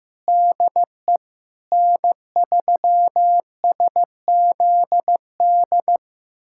Morse code, heard as a keyed tone, about 15 wpm, 700 Hz. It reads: DE N3SZD
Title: USA Callsign DE Callsign 15wpm